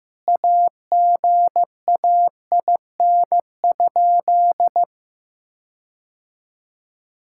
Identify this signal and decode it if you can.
Morse code, heard as a keyed tone, about 15 wpm, 700 Hz. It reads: AGAIN?
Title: Sentences from Top 1600 Words 15wpm